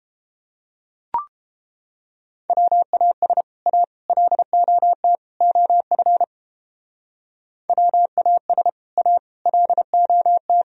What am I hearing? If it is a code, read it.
Morse code, heard as a keyed tone, about 30 wpm, 700 Hz. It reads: WAS A LOT OF WAS A LOT